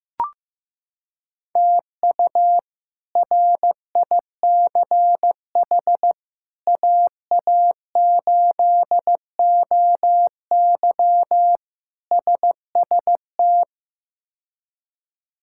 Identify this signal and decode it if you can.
Morse code, heard as a keyed tone, about 15 wpm, 700 Hz. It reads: TU RICH AA8OY SST
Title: K1USN SST QSO Practice - 15wpm